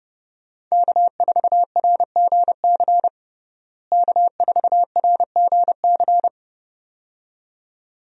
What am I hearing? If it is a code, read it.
Morse code, heard as a keyed tone, about 30 wpm, 700 Hz. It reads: K4RGC K4RGC
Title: USA Call Signs - 5 Characters - Repeated 2x - 30wpm